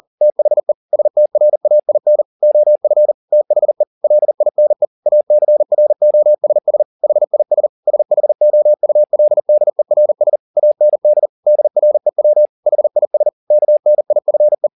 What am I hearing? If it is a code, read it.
Morse code, heard as a keyed tone, about 40 wpm, 600 Hz. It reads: THE STRAIN OF THE LINE ACROSS HIS SHOULDERS AND DREW HIS KNIFE
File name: the old man and the sea 2 40wpm